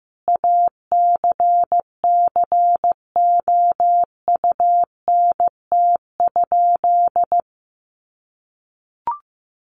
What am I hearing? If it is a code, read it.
Morse code, heard as a keyed tone, about 15 wpm, 700 Hz. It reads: ACCOUNT?
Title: Sentences from Top 1500 Words 15wpm